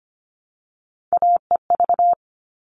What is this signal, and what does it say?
Morse code, heard as a keyed tone, about 25 wpm, 700 Hz. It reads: AEV
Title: All Cut Numbers - 3-Digit 25wpm